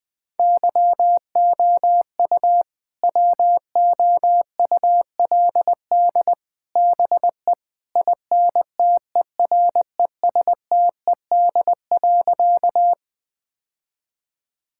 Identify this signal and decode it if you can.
Morse code, heard as a keyed tone, about 20 wpm, 700 Hz. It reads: YOU WOULD BE INTERESTED.
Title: Sentences from Top 1000 Words - Review Part 6 - 20wpm